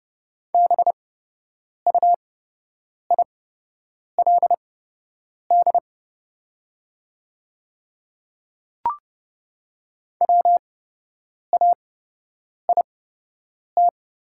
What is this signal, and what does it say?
Morse code, heard as a keyed tone, about 30 wpm, spaced as for 8 wpm, 700 Hz. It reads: BUILD WAIT
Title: 8x Character Spacing - Top 500 Words - 30wpm